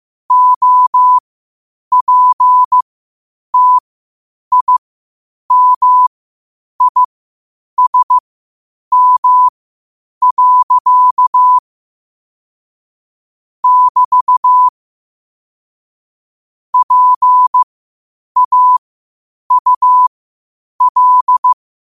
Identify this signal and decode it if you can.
Morse code, heard as a keyed tone, about 15 wpm, spaced as for 8 wpm, 1000 Hz. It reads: OPTIMISM. <BT> PAUL
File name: qotdfiles / 8wpm